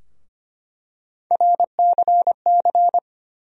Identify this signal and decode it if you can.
Morse code, heard as a keyed tone, about 25 wpm, 700 Hz. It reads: RCC
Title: USA Call Sign Postfixes - 3 Letters 25wpm